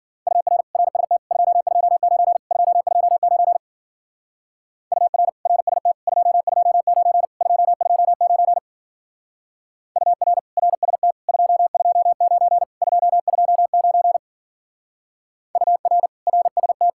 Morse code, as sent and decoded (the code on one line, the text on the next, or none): ..- .-. / .-. ... - / ..--- ..--- ----. / ..--- ..--- ----. / ..- .-. / .-. ... - / ..--- ..--- ----. / ..--- ..--- ----. / ..- .-. / .-. ... - / ..--- ..--- ----. / ..--- ..--- ----. / ..- .-. / .-. ... -
UR RST 229 229 UR RST 229 229 UR RST 229 229 UR RST